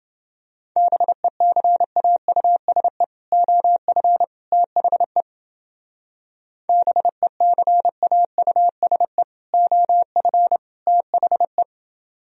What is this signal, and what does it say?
Morse code, heard as a keyed tone, about 30 wpm, 700 Hz. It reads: BECAUSE OF THE BECAUSE OF THE